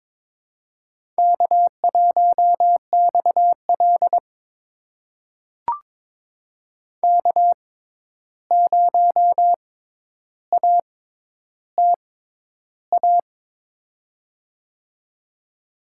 Morse code, heard as a keyed tone, about 22 wpm, spaced as for 11 wpm, 700 Hz. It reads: K1XL K0ATA
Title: USA Call Signs - 6x Character Spacing - 22wpm